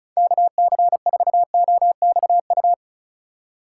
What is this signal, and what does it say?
Morse code, heard as a keyed tone, about 36 wpm, 690 Hz. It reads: KC4OXU
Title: USA Call Signs - 6 Characters - 4x Character Spacing - 35wpm